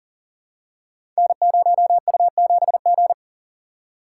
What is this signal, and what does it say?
Morse code, heard as a keyed tone, about 40 wpm, 690 Hz. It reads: N0U7G
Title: Code Groups - 5 Letter:Numbers - 40wpm